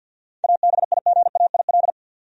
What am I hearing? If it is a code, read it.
Morse code, heard as a keyed tone, about 53 wpm, 700 Hz. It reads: ABIGAIL
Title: USA Common Names - 50wpm